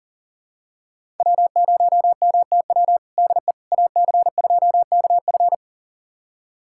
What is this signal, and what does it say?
Morse code, heard as a keyed tone, about 40 wpm, 700 Hz. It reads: W0MTW DE AC2KF
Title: USA Callsign DE Callsign - 5 Characters - 40wpm